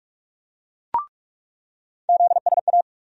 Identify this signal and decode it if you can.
Morse code, heard as a keyed tone, about 42 wpm, 700 Hz. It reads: ZSA
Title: USA Call Sign Postfixes - 3 Letters 45wpm